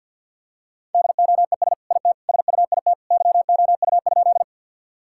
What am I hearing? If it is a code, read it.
Morse code, heard as a keyed tone, about 50 wpm, 700 Hz. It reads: DOES IT SUIT YOU?
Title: Sentences from Top 1600 Words 50wpm